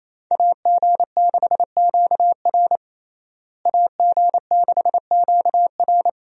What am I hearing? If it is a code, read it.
Morse code, heard as a keyed tone, about 28 wpm, 700 Hz. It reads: AG6QR AG6QR
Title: USA Call Signs - 5 Characters - Repeated 3x - 28wpm